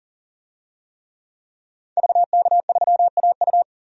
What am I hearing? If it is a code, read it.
Morse code, heard as a keyed tone, about 40 wpm, 700 Hz. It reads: VK3AU
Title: DXCC-100 Call Signs - 5 Characters - 40wpm